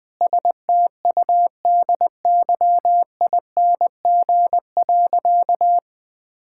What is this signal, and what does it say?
Morse code, heard as a keyed tone, about 20 wpm, 700 Hz. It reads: STUDYING.
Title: Difficult Sentences with Hints - 20wpm